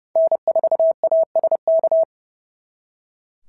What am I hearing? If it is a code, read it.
Morse code, heard as a keyed tone, about 30 wpm, 690 Hz. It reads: N4ASK
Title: SOTA QSO Practice 30wpm